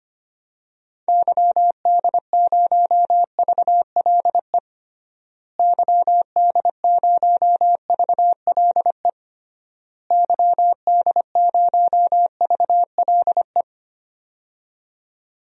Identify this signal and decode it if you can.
Morse code, heard as a keyed tone, about 25 wpm, 700 Hz. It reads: YD0VLE YD0VLE YD0VLE
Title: DXCC-100 Call Signs - 6 Characters - Repeated 3x - 25wpm